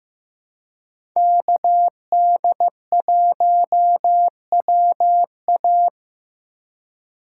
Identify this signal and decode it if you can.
Morse code, heard as a keyed tone, about 16 wpm, 700 Hz. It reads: KD1WA
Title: USA Call Signs - 4x Character Spacing - 15wpm